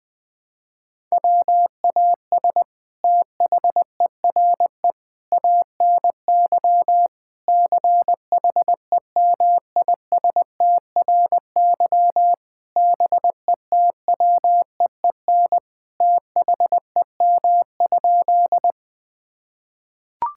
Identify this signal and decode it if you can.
Morse code, heard as a keyed tone, about 20 wpm, 700 Hz. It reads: WAS THERE ANY CHEMISTRY BETWEEN THEM?
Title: Sentences from Top 1000 Words - Review Part 6 - 20wpm